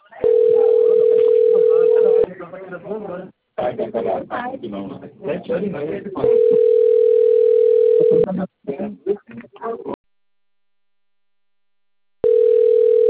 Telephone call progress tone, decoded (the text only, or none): ringback tone